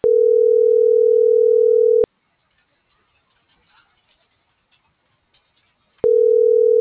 Ringback tone (North America), 440 + 480 Hz.
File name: CENTRAL TERRESTRE